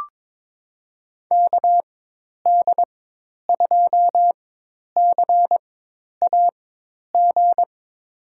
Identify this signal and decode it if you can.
Morse code, heard as a keyed tone, about 21 wpm, spaced as for 10 wpm, 700 Hz. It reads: KD2CAG
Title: USA Call Signs - 4x Character Spacing - 22wpm